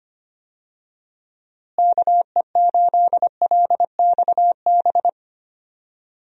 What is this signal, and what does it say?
Morse code, heard as a keyed tone, about 25 wpm, 700 Hz. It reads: KE8LXB
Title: USA Call Signs - 6 Characters - Repeated 2x - 25wpm